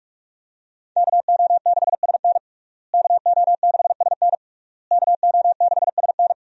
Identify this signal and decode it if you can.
Morse code, heard as a keyed tone, about 46 wpm, 700 Hz. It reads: KO6SN KO6SN KO6SN
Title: USA Call Signs - 5 Characters - Repeated 3x - 45wpm